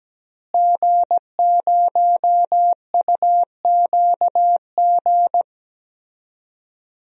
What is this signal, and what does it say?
Morse code, heard as a keyed tone, about 17 wpm, 700 Hz. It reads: G0UQG
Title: DXCC-100 Call Signs - 5 Characters - Repeated 2x - 17wpm